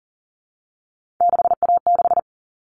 Morse code, heard as a keyed tone, about 40 wpm, 700 Hz. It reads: 6A6
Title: All Cut Numbers - 3-Digit 40wpm